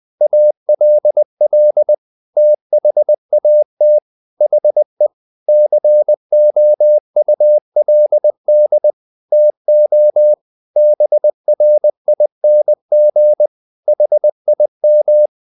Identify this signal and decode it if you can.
Morse code, heard as a keyed tone, about 20 wpm, 600 Hz. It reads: ALL THAT HE COULD TO BRING HIM
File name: the old man and the sea part 2 20wpm